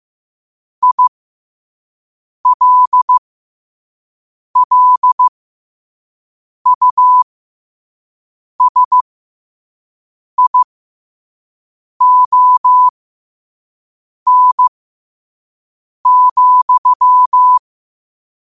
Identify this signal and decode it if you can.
Morse code, heard as a keyed tone, about 15 wpm, spaced as for 5 wpm, 1000 Hz. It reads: ILLUSION,
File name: qotdfiles / 5wpm